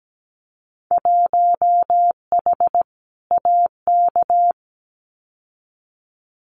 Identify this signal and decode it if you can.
Morse code, heard as a keyed tone, about 17 wpm, 700 Hz. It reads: 1H AK
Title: Winter Field Day Exchanges Encore 17wpm